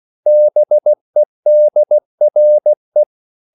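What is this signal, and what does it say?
Morse code, heard as a keyed tone, about 16 wpm, 600 Hz. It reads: BEDRE
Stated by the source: Slow (QRS) Morse code at 16 wpm, from CW med Gnister.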